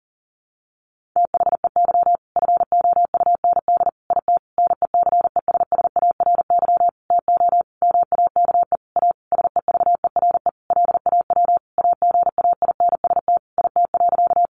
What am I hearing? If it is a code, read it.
Morse code, heard as a keyed tone, about 40 wpm, 700 Hz. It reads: THEY FOUND IT NECESSARY TO MAKE A SEVERE LAW AGAINST IT.
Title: Sentences from Top 1400 Words 40wpm